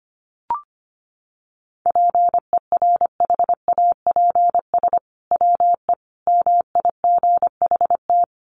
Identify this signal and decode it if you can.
Morse code, heard as a keyed tone, about 25 wpm, 700 Hz. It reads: PERHAPS WE MIGHT
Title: Sentences from Top 1500 Words 25wpm